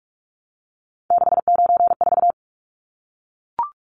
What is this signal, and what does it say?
Morse code, heard as a keyed tone, about 44 wpm, 700 Hz. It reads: 694
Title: Numbers - 3-Digit 45wpm